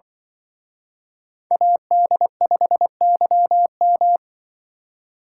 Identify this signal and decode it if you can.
Morse code, heard as a keyed tone, about 25 wpm, 700 Hz. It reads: AD5YM
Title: Speed Racing - USA Call Signs 20wpm